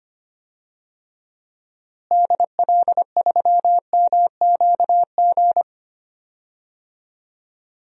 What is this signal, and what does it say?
Morse code, heard as a keyed tone, about 25 wpm, 700 Hz. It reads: DL3MQG
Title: DXCC-100 Call Signs - 6 Characters - Repeated 3x - 25wpm